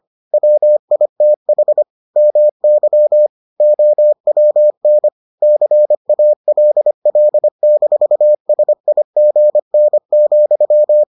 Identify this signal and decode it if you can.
Morse code, heard as a keyed tone, about 25 wpm, 600 Hz. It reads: WITH MY OWN CALL-SIGN,